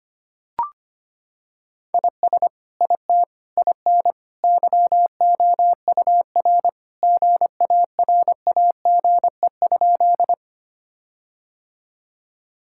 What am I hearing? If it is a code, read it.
Morse code, heard as a keyed tone, about 25 wpm, 700 Hz. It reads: IS IT IN YOUR GARAGE?
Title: Sentences from Top 1600 Words 25wpm